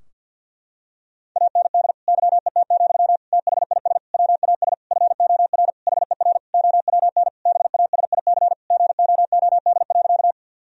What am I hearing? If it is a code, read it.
Morse code, heard as a keyed tone, about 51 wpm, 700 Hz. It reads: AND YET, THIS WAS FOR HER OWN BASIC GOOD.
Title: Sentences from Top 1000 Words - Review Part 3 - 50wpm